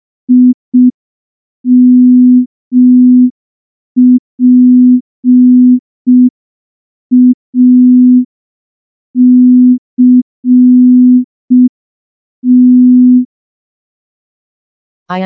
Morse code, heard as a keyed tone, about 5 wpm, 250 Hz. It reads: IMPACT
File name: morse / morse-training-05WPM